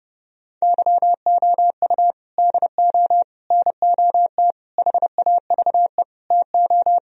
Morse code, heard as a keyed tone, about 31 wpm, 700 Hz. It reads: YOU DO NOT HAVE TO